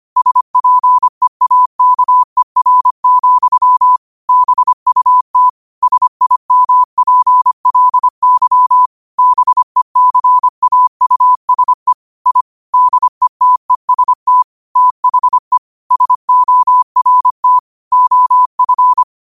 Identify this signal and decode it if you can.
Morse code, heard as a keyed tone, about 25 wpm, 1000 Hz. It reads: SPEAKER, BUT SIMPLY BECAUSE I DETEST THE SORT OF
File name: qotdfiles / 25wpm